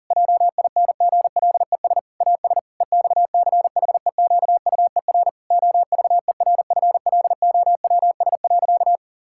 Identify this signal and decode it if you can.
Morse code, heard as a keyed tone, about 40 wpm, 700 Hz. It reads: JINGLES AS EXCHEQUER OVERFLOWS.
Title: Pangram Challenge v2 - 40wpm